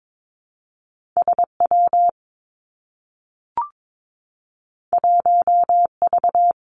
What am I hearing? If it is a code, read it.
Morse code, heard as a keyed tone, about 22 wpm, 700 Hz. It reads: S W 1 V
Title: Two Challenging Letter-Number Combinations 22wpm